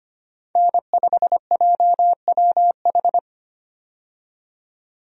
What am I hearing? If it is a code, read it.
Morse code, heard as a keyed tone, about 25 wpm, 690 Hz. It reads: N5JWH
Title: USA Call Signs - 5 Characters - 25wpm